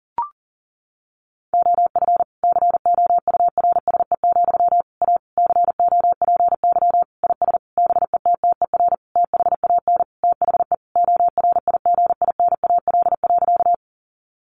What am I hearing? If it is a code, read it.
Morse code, heard as a keyed tone, about 37 wpm, 700 Hz. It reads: OF COURSE, A COPY IS BETTER THAN THE ORIGINAL.
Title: Sentences from Top 1400 Words 40wpm